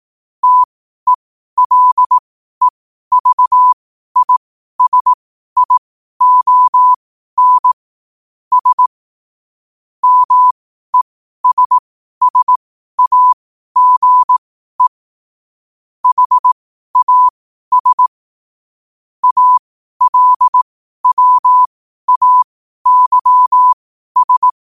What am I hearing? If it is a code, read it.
Morse code, heard as a keyed tone, about 18 wpm, spaced as for 12 wpm, 1000 Hz. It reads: TELEVISION S MESSAGE HAS ALWAYS
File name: qotdfiles / 12wpm